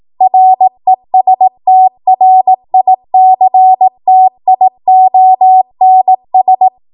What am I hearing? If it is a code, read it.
Morse code, heard as a keyed tone, about 18 wpm, 750 Hz. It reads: RESTRICTIONS
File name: Morse / 18WPM